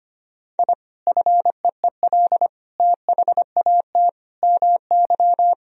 Morse code, heard as a keyed tone, about 25 wpm, 700 Hz. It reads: I FEEL THAT MY
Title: Sentences from Top 1000 Words - Review Part 1 - 25wpm